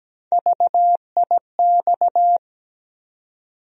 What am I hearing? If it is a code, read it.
Morse code, heard as a keyed tone, about 17 wpm, 700 Hz. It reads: VIX
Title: USA Call Sign Postfixes - 3 Letters 17wpm